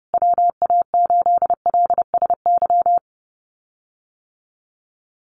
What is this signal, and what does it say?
Morse code, heard as a keyed tone, about 30 wpm, 700 Hz. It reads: WA8LSY
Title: USA Call Signs Repeated 3x 30wpm